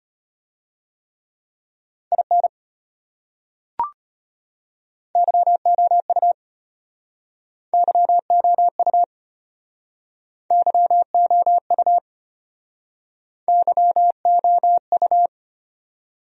Morse code, heard as a keyed tone, about 33 wpm, 700 Hz. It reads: IN YOU YOU YOU YOU